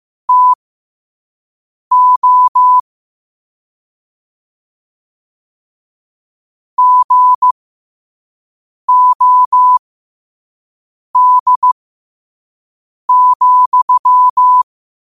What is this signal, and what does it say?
Morse code, heard as a keyed tone, about 15 wpm, spaced as for 5 wpm, 1000 Hz. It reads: TO GOD,
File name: qotdfiles / 5wpm